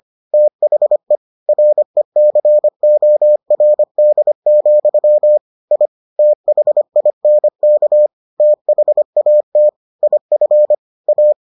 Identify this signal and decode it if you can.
Morse code, heard as a keyed tone, about 25 wpm, 600 Hz. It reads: THE RECORD, I THINK THAT IF A